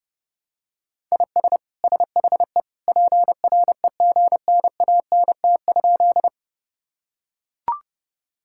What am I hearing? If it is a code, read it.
Morse code, heard as a keyed tone, about 32 wpm, 700 Hz. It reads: IS SHE PREGNANT?
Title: Sentences from Top 1000 Words - Review Part 6 - 30wpm